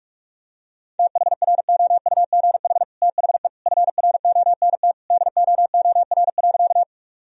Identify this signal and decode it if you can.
Morse code, heard as a keyed tone, about 46 wpm, 700 Hz. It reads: THROUGH THE FRONT DOOR.
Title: Sentences from Top 1000 Words - Review Part 2 - 45wpm